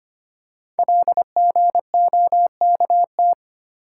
Morse code, heard as a keyed tone, about 25 wpm, 700 Hz. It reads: LGOKT
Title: Code Groups - 5 Letters - 25wpm